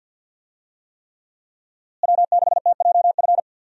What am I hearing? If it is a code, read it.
Morse code, heard as a keyed tone, about 50 wpm, 700 Hz. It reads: W6TJF
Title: USA Call Signs - 5 Characters - Repeated 3x - 50wpm